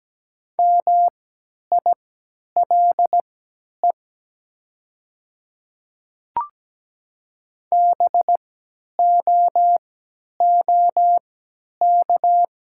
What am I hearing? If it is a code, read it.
Morse code, heard as a keyed tone, about 17 wpm, spaced as for 10 wpm, 700 Hz. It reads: MILE BOOK
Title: 3x Character Spacing - Top 500 Words - 17wpm